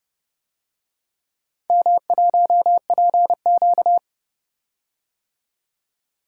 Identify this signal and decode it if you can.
Morse code, heard as a keyed tone, about 30 wpm, 700 Hz. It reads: M1PQ